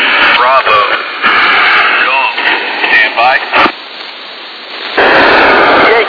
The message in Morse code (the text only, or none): none